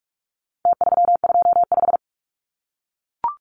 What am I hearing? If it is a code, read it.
Morse code, heard as a keyed tone, about 44 wpm, 700 Hz. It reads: T325
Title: Numbers - Contest Style - 4-Digit 45wpm